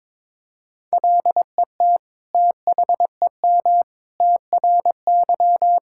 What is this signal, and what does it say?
Morse code, heard as a keyed tone, about 22 wpm, 700 Hz. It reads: LET THEM TRY
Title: Sentences from Top 1000 Words - Review Part 5 - 22wpm